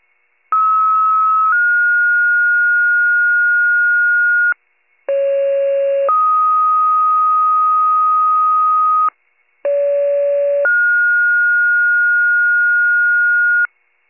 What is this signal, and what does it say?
Two-tone page, heard as a one-second tone, then a three-second tone, about 1290 then 1430 Hz, about 550 then 1250 Hz, about 570 then 1430 Hz, with noise. It is heard from 0.5 to 13.7 s.
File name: pager / 78 RESCUE / wcalert